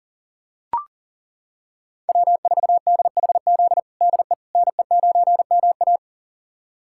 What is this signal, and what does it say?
Morse code, heard as a keyed tone, about 40 wpm, 700 Hz. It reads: W4DHZ DE NE9MA
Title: USA Callsign DE Callsign - 5 Characters - 40wpm